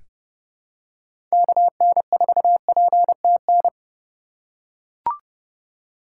Morse code, heard as a keyed tone, about 30 wpm, 700 Hz. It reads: KN4PTN
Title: USA Call Signs - 6 Characters - 3x Character Spacing - 30wpm